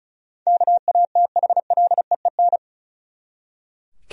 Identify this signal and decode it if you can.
Morse code, heard as a keyed tone, about 36 wpm, 700 Hz. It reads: KATHLEEN
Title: USA Common Names - 35wpm